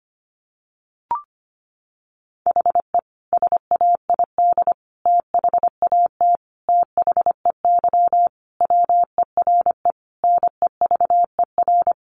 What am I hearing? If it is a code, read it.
Morse code, heard as a keyed tone, about 25 wpm, 700 Hz. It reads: HE SAID THAT THEY WERE NEVER
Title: Sentences from Top 800 Words 25wpm